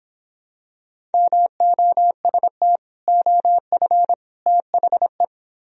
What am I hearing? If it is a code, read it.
Morse code, heard as a keyed tone, about 26 wpm, 700 Hz. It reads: MOST OF THE